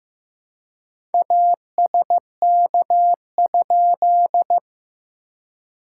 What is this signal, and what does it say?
Morse code, heard as a keyed tone, about 15 wpm, 700 Hz. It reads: ASK?
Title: SOTA QSO Practice 15wpm